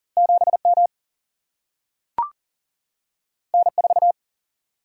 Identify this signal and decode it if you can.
Morse code, heard as a keyed tone, about 40 wpm, 700 Hz. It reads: 7 M N 4
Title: Single-Letter Single-Number 40wpm